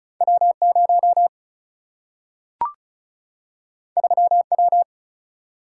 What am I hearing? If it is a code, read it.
Morse code, heard as a keyed tone, about 34 wpm, 700 Hz. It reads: W0 3W
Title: Single-Letter Single-Number 35wpm